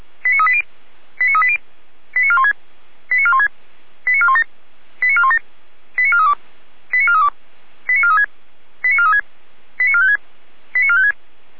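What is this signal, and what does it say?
Five-tone selective call (ZVEI), heard as quick runs of five tones, tones of about 70 ms. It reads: 89380 89380 89416 89416 89417 89417 89432 89432 89436 89436 89456 89456
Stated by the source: Five-tone ZVEI selective call received by SDR, feuerwehr-orlamuende fire brigade server.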